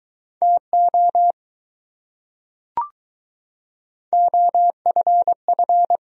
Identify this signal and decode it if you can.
Morse code, heard as a keyed tone, about 21 wpm, 700 Hz. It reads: TO OFF